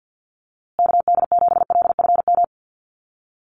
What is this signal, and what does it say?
Morse code, heard as a keyed tone, about 50 wpm, 700 Hz. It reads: KD7LFM
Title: USA Call Signs Repeated 2x 50wpm